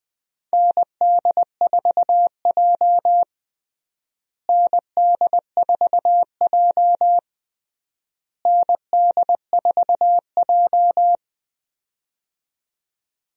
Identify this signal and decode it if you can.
Morse code, heard as a keyed tone, about 20 wpm, 700 Hz. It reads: ND4J ND4J ND4J
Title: USA Call Signs - 4 Characters - Repeated 3x - 20wpm